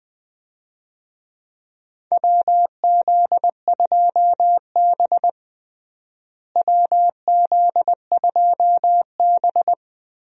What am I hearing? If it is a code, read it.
Morse code, heard as a keyed tone, about 21 wpm, 700 Hz. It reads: WZ2B WZ2B
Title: USA Call Signs - 4 Characters - Repeated 2x - 20wpm